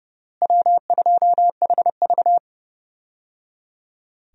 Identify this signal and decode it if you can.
Morse code, heard as a keyed tone, about 30 wpm, 700 Hz. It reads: W2HV
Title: USA Call Signs - 4 Characters - Repeated 3x - 30wpm